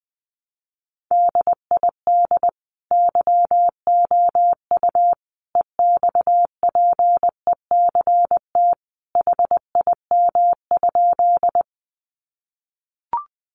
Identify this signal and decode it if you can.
Morse code, heard as a keyed tone, about 20 wpm, 700 Hz. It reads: DID YOU EXPECT HIM?
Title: Sentences from Top 1100 Words 20wpm